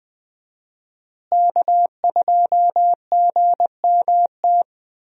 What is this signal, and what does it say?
Morse code, heard as a keyed tone, about 20 wpm, 700 Hz. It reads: K2GMT
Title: USA Call Signs - 5 Characters - Repeated 3x - 20wpm